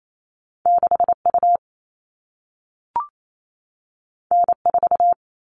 Morse code, heard as a keyed tone, about 28 wpm, 700 Hz. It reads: 6U N4
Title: All Cut Numbers - 2-Digit 28wpm